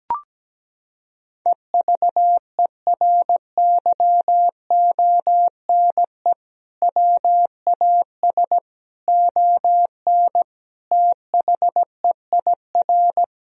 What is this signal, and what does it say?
Morse code, heard as a keyed tone, about 17 wpm, 700 Hz. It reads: EVERYONE WAS ON THEIR
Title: Sentences from Top 1000 Words - Review Part 1 - 17wpm